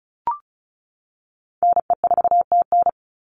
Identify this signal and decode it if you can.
Morse code, heard as a keyed tone, about 34 wpm, 700 Hz. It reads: NE4TN
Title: NA SOTA Chaser Callsigns 35wpm